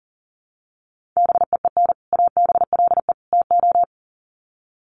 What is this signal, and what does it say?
Morse code, heard as a keyed tone, about 37 wpm, 700 Hz. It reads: BEEN ABLE TO
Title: Sets of 3 Words 40wpm